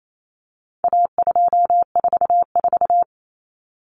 Morse code, heard as a keyed tone, about 28 wpm, 700 Hz. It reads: A244
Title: Common Cut Numbers - 4-Digit 28wpm